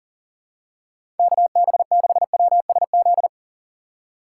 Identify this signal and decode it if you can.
Morse code, heard as a keyed tone, about 40 wpm, 700 Hz. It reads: KB6WSZ
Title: USA Call Signs - 6 Characters - 4x Character Spacing - 40wpm